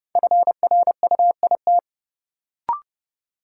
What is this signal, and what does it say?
Morse code, heard as a keyed tone, about 30 wpm, 700 Hz. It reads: FRUIT
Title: Sentences from Top 1300 Words - New Word Drill - 30wpm